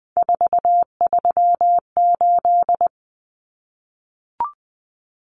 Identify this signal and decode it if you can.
Morse code, heard as a keyed tone, about 20 wpm, 700 Hz. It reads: 438
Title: Common Cut Numbers - 3-Digit 20wpm